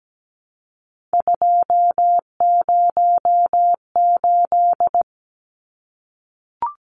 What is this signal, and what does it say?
Morse code, heard as a keyed tone, about 17 wpm, 700 Hz. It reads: 208
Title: Numbers - 3-Digit 17wpm